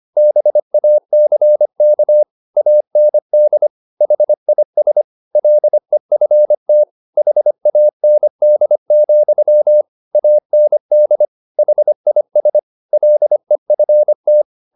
Morse code, heard as a keyed tone, about 25 wpm, 600 Hz. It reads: BACK AND HIS LEFT HAND, AND HIS LEFT